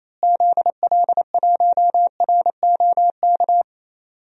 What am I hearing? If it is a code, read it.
Morse code, heard as a keyed tone, about 28 wpm, 700 Hz. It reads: ZL1ROK
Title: DXCC-100 Call Signs - 6 Characters - Repeated 3x - 28wpm